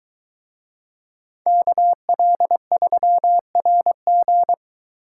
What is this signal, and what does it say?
Morse code, heard as a keyed tone, about 22 wpm, 700 Hz. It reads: KL3RG